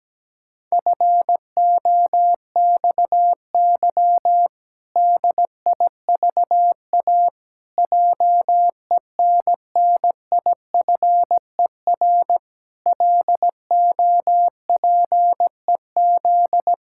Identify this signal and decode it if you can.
Morse code, heard as a keyed tone, about 17 wpm, 700 Hz. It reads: FOXY DIVA JENNIFER LOPEZ
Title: Pangram Challenge v2 - 17wpm